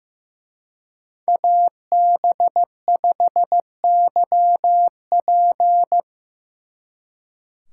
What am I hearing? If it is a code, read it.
Morse code, heard as a keyed tone, about 18 wpm, 700 Hz. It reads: AB5YP
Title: Speed Racing - USA 2x2 Call Signs 15wpm